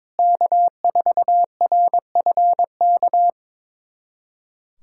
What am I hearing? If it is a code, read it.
Morse code, heard as a keyed tone, about 22 wpm, 700 Hz. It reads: K4RFK
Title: USA Callsign DE Callsign - 5 Characters - 22wpm